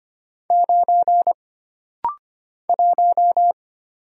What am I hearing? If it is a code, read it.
Morse code, heard as a keyed tone, about 25 wpm, 700 Hz. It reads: 9 1
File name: Single Character (Letter:Number:Punctuation:Prosign) - Rapid-Fire - 25wpm